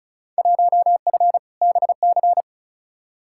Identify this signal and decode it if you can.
Morse code, heard as a keyed tone, about 36 wpm, 700 Hz. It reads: 1F BC
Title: Field Day Exchanges Encore - 35wpm